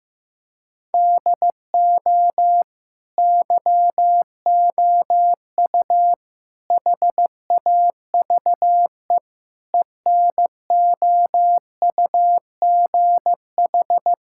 Morse code, heard as a keyed tone, about 16 wpm, 700 Hz. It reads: DO YOU HAVE ENOUGH
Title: Sentences from Top 1000 Words - Review Part 5 - 15wpm